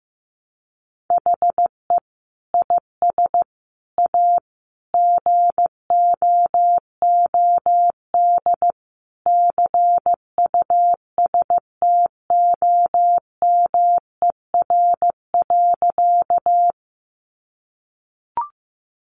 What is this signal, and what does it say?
Morse code, heard as a keyed tone, about 15 wpm, 700 Hz. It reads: HE IS A GOOD CUSTOMER.
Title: Sentences from Top 1100 Words 15wpm